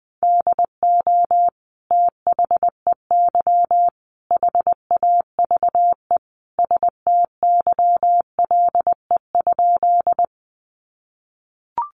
Morse code, heard as a keyed tone, about 20 wpm, 700 Hz. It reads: DO THEY HAVE STYLE?
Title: Sentences from Top 800 Words 20wpm